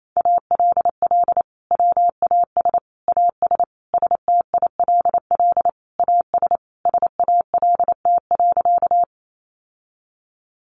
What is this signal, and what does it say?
Morse code, heard as a keyed tone, about 28 wpm, 700 Hz. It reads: ALL WAS AS STILL AS SALT.
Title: Sentences from Top 800 Words 28wpm